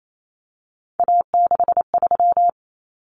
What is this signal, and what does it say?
Morse code, heard as a keyed tone, about 28 wpm, 700 Hz. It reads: A63
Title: Common Cut Numbers - 3-Digit 28wpm